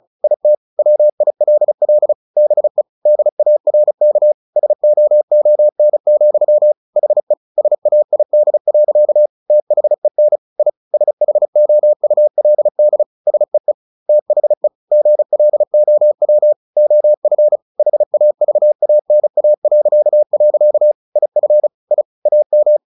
Morse code, heard as a keyed tone, about 34 wpm, 600 Hz. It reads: IT WILL BE DARK SOON, HE SAID. THEN I SHOULD SEE THE GLOW OF HAVANA.. IF I AM
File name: the old man and the sea 2 35wpm